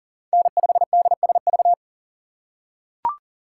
Morse code, heard as a keyed tone, about 40 wpm, 700 Hz. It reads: N5DSV
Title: USA Call Signs - 5 Characters - 4x Character Spacing - 40wpm